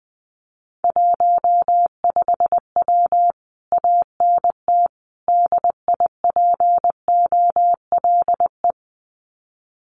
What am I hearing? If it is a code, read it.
Morse code, heard as a keyed tone, about 19 wpm, 700 Hz. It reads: 15W ANT DIPOLE
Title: Common QSO Phrases Encore 20wpm